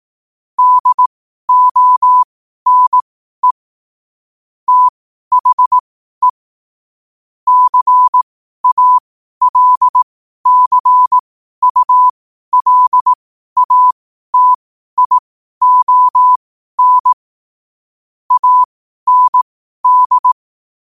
Morse code, heard as a keyed tone, about 18 wpm, spaced as for 12 wpm, 1000 Hz. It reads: DONE THE CALCULATION AND
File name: 12wpm